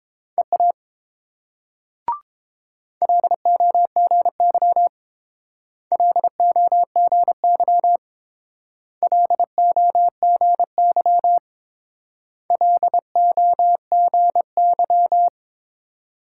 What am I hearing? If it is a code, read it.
Morse code, heard as a keyed tone, about 28 wpm, 700 Hz. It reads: EA LOGY LOGY LOGY LOGY